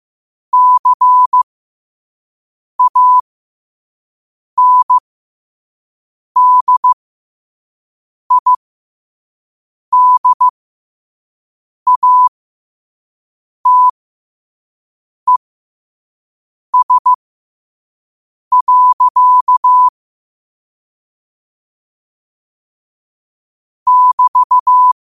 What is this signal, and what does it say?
Morse code, heard as a keyed tone, about 15 wpm, spaced as for 5 wpm, 1000 Hz. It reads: CANDIDATES. <BT>